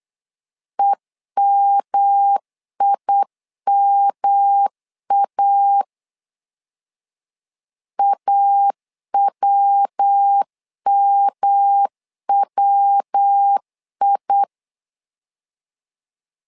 Morse code, heard as a keyed tone, about 8 wpm, 800 Hz. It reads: EMIMA AWMWI
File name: LEZIONI / 3 MWRO / 3 MW 8wpm 2501